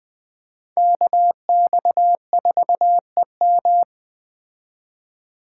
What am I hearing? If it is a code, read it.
Morse code, heard as a keyed tone, about 21 wpm, 700 Hz. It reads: KX4EM